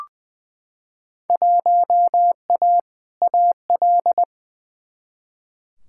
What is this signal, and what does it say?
Morse code, heard as a keyed tone, about 21 wpm, 700 Hz. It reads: 1A AL
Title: Field Day Exchanges - 20wpm